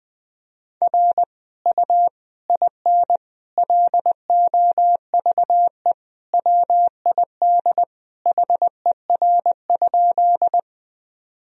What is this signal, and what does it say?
Morse code, heard as a keyed tone, about 21 wpm, 700 Hz. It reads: R U IN LOVE WID HER?
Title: Sentences from Top 400 Words and Abbreviations 20wpm